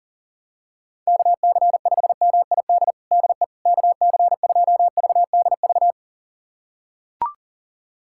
Morse code, heard as a keyed tone, about 40 wpm, 700 Hz. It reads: KC5MID DE KC2VDV
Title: USA Callsign DE Callsign - 6 Characters - 40wpm